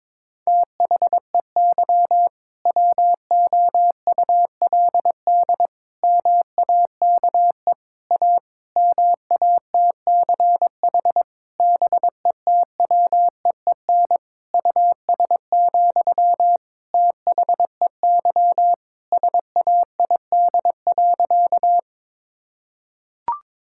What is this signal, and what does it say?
Morse code, heard as a keyed tone, about 22 wpm, 700 Hz. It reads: THEY WOULD MAKE A MATCH BETWEEN US, THEY SAID.
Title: Sentences from Top 1600 Words 22wpm